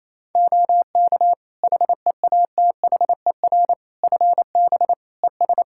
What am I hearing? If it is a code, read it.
Morse code, heard as a keyed tone, about 27 wpm, 700 Hz. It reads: OK HEATHER FB ES
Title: K1USN SST QSO Practice 28wpm